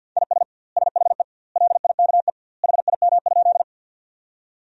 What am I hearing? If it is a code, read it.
Morse code, heard as a keyed tone, about 50 wpm, 700 Hz. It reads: IS SHE LIKE HIM?
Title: 2x Word-Spacing - Sentences from Top 100 Words - 50wpm